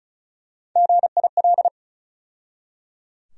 Morse code, heard as a keyed tone, about 36 wpm, 700 Hz. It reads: GIL
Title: USA Call Sign Postfixes - 3 Letters 35wpm